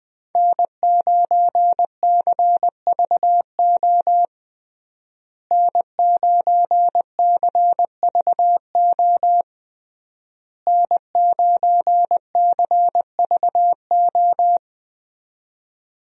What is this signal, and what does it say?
Morse code, heard as a keyed tone, about 21 wpm, 700 Hz. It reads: N9CVO N9CVO N9CVO
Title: USA Call Signs - 5 Characters - Repeated 3x - 20wpm